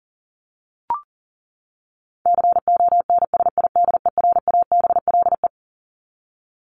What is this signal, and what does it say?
Morse code, heard as a keyed tone, about 40 wpm, 700 Hz. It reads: CONSIDERABLE
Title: LY Suffix - Top 100 Words with Base Words 40wpm